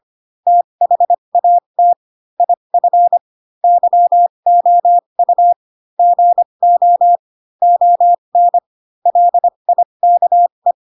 Morse code, heard as a keyed tone, about 25 wpm, spaced as for 22 wpm, 700 Hz. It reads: THAT IF YOU GO ON LIKE
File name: Pinocchio / Pinocchio ch4 18 25 ik1hge